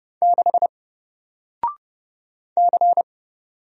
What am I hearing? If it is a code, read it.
Morse code, heard as a keyed tone, about 30 wpm, 700 Hz. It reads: 6 C
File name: Single-Letter-Number-30wpm